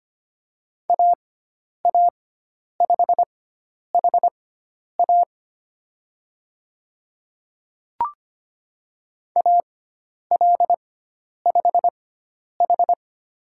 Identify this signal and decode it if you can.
Morse code, heard as a keyed tone, about 25 wpm, spaced as for 10 wpm, 700 Hz. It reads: AA5HA AL5H
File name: Lesson 16g - Letter c - Call Signs 25wpm